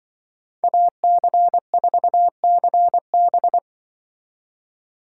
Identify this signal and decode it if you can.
Morse code, heard as a keyed tone, about 25 wpm, 700 Hz. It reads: AC4CB
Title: Speed Racing - USA 2x2 Call Signs 20wpm